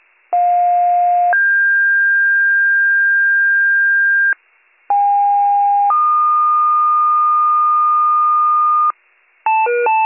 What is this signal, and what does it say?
Two-tone page, heard as a one-second tone, then a three-second tone, about 710 then 1600 Hz, about 810 then 1220 Hz, with noise. It is heard from 0.3 to 8.9 s.